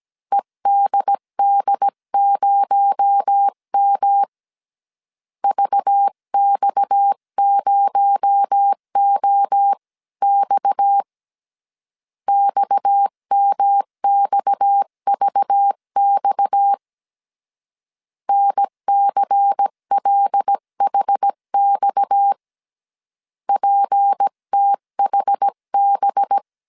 Morse code, heard as a keyed tone, about 17 wpm, 800 Hz. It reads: EDD0M VX0OX XMXVX NCLHX PTHB
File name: CW / DET / 1 6 1500C S C 16wpm 2733